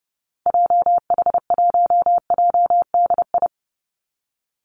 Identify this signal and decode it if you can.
Morse code, heard as a keyed tone, about 28 wpm, 700 Hz. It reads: JH1JDI
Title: Call Signs Repeated 2x - 30wpm